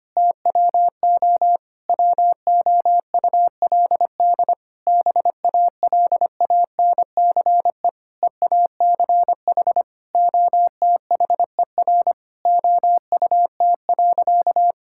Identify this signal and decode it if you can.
Morse code, heard as a keyed tone, about 25 wpm, 700 Hz. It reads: TWO WOULD BALANCE EACH OTHER OUT.
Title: Sentences from Top 1000 Words - Review Part 6 - 25wpm